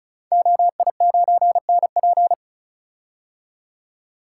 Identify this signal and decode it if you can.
Morse code, heard as a keyed tone, about 36 wpm, 690 Hz. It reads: OI9NP
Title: Code Groups - 5 Letter:Numbers - 35wpm